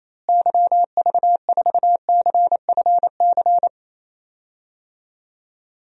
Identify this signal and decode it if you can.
Morse code, heard as a keyed tone, about 28 wpm, 700 Hz. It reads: YV4CFC
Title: DXCC-100 Call Signs - 6 Characters - Repeated 2x - 28wpm